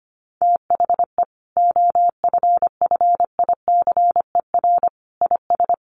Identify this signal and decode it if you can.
Morse code, heard as a keyed tone, about 25 wpm, 700 Hz. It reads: THE OFFICER IS
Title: Sentences from Top 1400 Words 25wpm